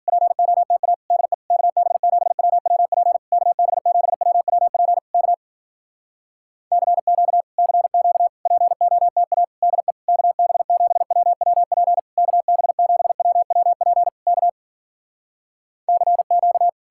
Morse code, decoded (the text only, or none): POTA DE KB7WWP KB7WWP K CQ CQ POTA DE KB7WWP KB7WWP K CQ